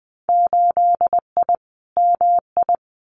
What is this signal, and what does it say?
Morse code, heard as a keyed tone, about 20 wpm, 700 Hz. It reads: 8I MI
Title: Winter Field Day Exchanges Encore 20wpm